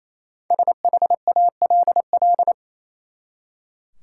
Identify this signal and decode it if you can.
Morse code, heard as a keyed tone, about 28 wpm, 700 Hz. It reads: SHALL
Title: Five Letter Words - 28wpm